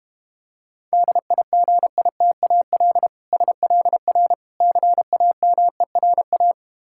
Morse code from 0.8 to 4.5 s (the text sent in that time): DIGITAL SLR